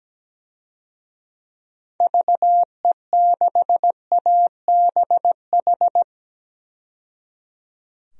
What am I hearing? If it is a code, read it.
Morse code, heard as a keyed tone, about 17 wpm, 700 Hz. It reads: VE6ABH
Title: DXCC-100 Call Signs - 6 Characters - 17wpm